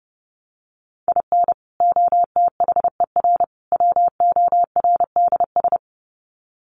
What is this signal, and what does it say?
Morse code, heard as a keyed tone, about 30 wpm, 700 Hz. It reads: IN OTHER WORDS
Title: Sets of 3 Words 30wpm